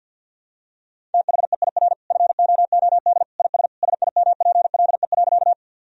Morse code, heard as a keyed tone, about 51 wpm, 700 Hz. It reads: THEIR FOOD IS SIMPLE.
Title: Sentences from Top 1000 Words - Review Part 1 - 50wpm